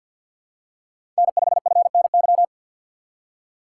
Morse code, heard as a keyed tone, about 47 wpm, 700 Hz. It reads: N5FNY
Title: USA Call Signs - 5 Characters - 2x Character Spacing - 50wpm